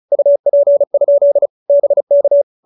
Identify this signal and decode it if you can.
Morse code, heard as a keyed tone, about 34 wpm, 590 Hz. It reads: UP? BK
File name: QSO312 35